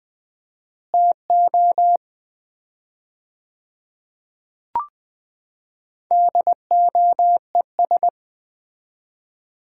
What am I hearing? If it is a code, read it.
Morse code, heard as a keyed tone, about 20 wpm, 700 Hz. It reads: TO DOES
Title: 4x Word-Spacing - Sets of 5 Words - 20wpm